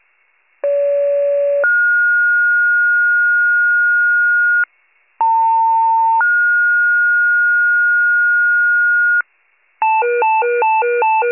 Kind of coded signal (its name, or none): two-tone page